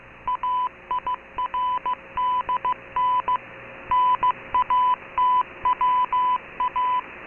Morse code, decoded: AIRDN NATWA